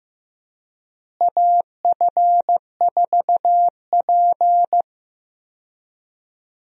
Morse code, heard as a keyed tone, about 15 wpm, 700 Hz. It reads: AF4P
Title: USA Call Signs - 4 Characters - Repeated 3x - 15wpm